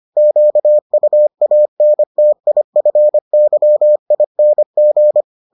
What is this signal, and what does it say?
Morse code, heard as a keyed tone, about 25 wpm, 600 Hz. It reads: QUANTIFYING